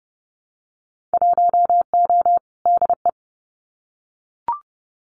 Morse code, heard as a keyed tone, about 31 wpm, 700 Hz. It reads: 1O DE
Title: Winter Field Day Exchanges 30wpm